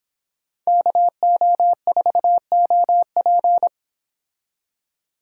Morse code, heard as a keyed tone, about 26 wpm, 700 Hz. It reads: KO4OP